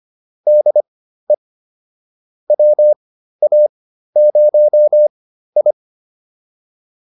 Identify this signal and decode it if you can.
Morse code, heard as a keyed tone, about 25 wpm, spaced as for 13 wpm, 600 Hz. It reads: DE WA0I